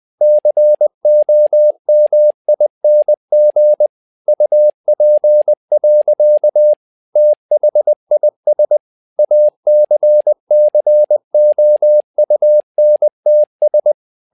Morse code, heard as a keyed tone, about 20 wpm, 600 Hz. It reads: COMING UP. THIS ACCOUNTS
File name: tower trouble 20wpm